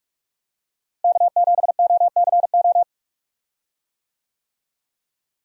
Morse code, heard as a keyed tone, about 46 wpm, 700 Hz. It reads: K7OCO
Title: USA Call Signs - 5 Characters - Repeated 3x - 45wpm